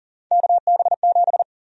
Morse code, heard as a keyed tone, about 40 wpm, 700 Hz. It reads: KB7
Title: USA Call Sign Prefixes - Repeated 2x - 40wpm